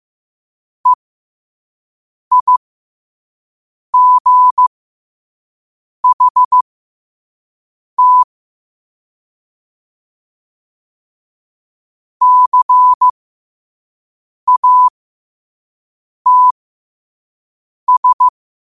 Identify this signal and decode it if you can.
Morse code, heard as a keyed tone, about 15 wpm, spaced as for 5 wpm, 1000 Hz. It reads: EIGHT CATS
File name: qotdfiles / 5wpm